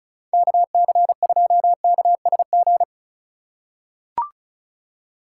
Morse code, heard as a keyed tone, about 34 wpm, 700 Hz. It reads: KC2KSG
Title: USA Call Signs - 6 Characters - 4x Character Spacing - 35wpm